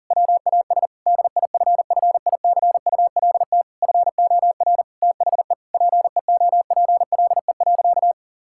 Morse code, heard as a keyed tone, about 40 wpm, 700 Hz. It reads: WAS DIFFICULT FOR THE PEOPLE.
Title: Sentences from Top 1000 Words - Review Part 5 - 40wpm